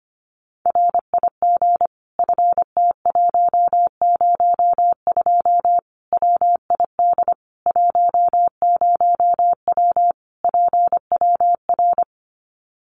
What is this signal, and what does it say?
Morse code, heard as a keyed tone, about 25 wpm, 700 Hz. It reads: RIG FT102 WID 10W PWR
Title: Common QSO Phrases 25wpm